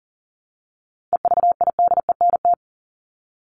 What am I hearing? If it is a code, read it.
Morse code, heard as a keyed tone, about 40 wpm, 700 Hz. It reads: EVIDENT
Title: LY Suffix - Top 100 Words with Base Words 40wpm